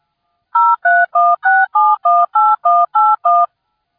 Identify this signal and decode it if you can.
DTMF, heard as telephone touch tones, tones of about 220 ms. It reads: 0316718181